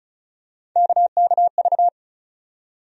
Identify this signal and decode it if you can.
Morse code, heard as a keyed tone, about 34 wpm, 700 Hz. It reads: KKV
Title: USA Call Sign Postfixes - 3 Letters 35wpm